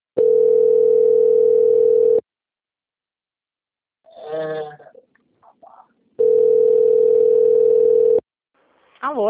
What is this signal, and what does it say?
Ringback tone (North America), 440 + 480 Hz, with speech.
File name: out- - - - - .17709